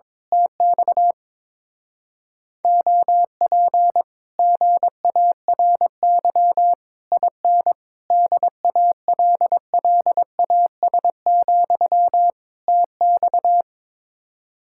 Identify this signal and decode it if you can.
Morse code, heard as a keyed tone, about 22 wpm, 700 Hz. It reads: TX OP GARY IN DALLAS, TX
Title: Speed-Racing Common QSO Phrases - 22wpm